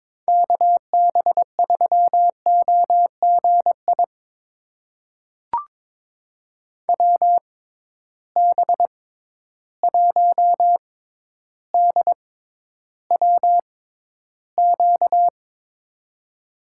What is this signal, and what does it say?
Morse code, heard as a keyed tone, about 22 wpm, spaced as for 11 wpm, 700 Hz. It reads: KB3OGI WB1DWQ
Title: USA Call Signs - 6 Characters - 6x Character Spacing - 22wpm